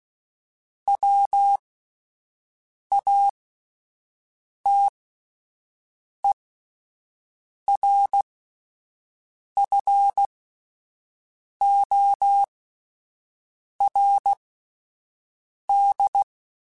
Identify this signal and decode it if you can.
Morse code, heard as a keyed tone, about 16 wpm, spaced as for 5 wpm, 800 Hz. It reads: WATERFORD